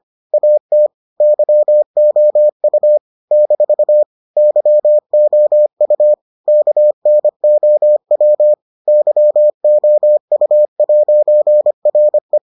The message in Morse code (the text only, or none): AT YOU - YOU KNOW YOU'RE